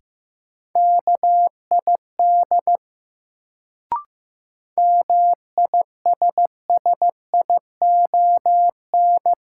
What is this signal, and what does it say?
Morse code, heard as a keyed tone, about 15 wpm, 700 Hz. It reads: KID MISSION
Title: Sentences from Top 1600 Words - New Word Drill Alt - 15wpm